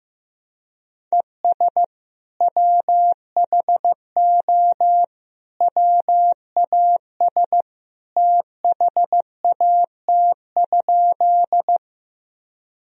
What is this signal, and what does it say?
Morse code, heard as a keyed tone, about 15 wpm, 700 Hz. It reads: ES WHO WAS THAT?
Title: Sentences from Top 100 Words and Abbreviations 15wpm